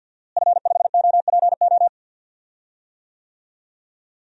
Morse code, heard as a keyed tone, about 52 wpm, 700 Hz. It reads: F5OPO
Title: DXCC-100 Call Signs - 5 Characters - Repeated 3x - 50wpm